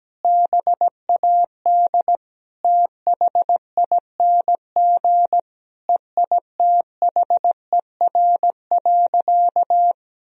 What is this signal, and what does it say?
Morse code, heard as a keyed tone, about 17 wpm, 700 Hz. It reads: BAD THING EITHER.
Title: Sentences from Top 1000 Words - Review Part 4 - 17wpm